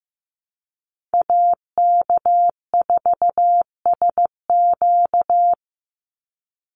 Morse code, heard as a keyed tone, about 15 wpm, 700 Hz. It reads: AK4SQ
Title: Call Signs Repeated 2x - 15wpm